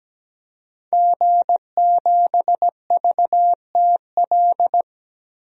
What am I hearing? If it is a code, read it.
Morse code, heard as a keyed tone, about 17 wpm, 700 Hz. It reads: G7VTL
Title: DXCC-100 Call Signs - 5 Characters - 17wpm